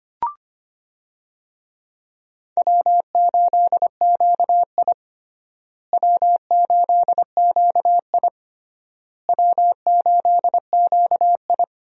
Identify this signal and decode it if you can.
Morse code, heard as a keyed tone, about 25 wpm, 700 Hz. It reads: W8QI W8QI W8QI
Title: USA Call Signs - 4 Characters - Repeated 3x - 25wpm